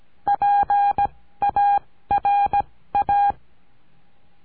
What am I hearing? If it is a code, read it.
Morse code, heard as a keyed tone, about 17 wpm, spaced as for 14 wpm, 810 Hz. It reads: PARA
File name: CURSO DE TELEGRAFIA